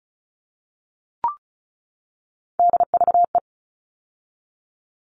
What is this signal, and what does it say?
Morse code, heard as a keyed tone, about 34 wpm, 700 Hz. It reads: DVE
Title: All Cut Numbers - 3-Digit 35wpm